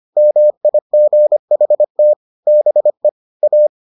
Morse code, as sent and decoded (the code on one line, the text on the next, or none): -- .. --. .... - / -... / .-
MIGHT BE A